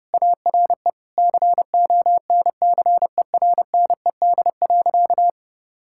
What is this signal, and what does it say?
Morse code, heard as a keyed tone, about 30 wpm, 700 Hz. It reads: ARE CONCERNED.
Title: Sentences from Top 1000 Words - Review Part 6 - 30wpm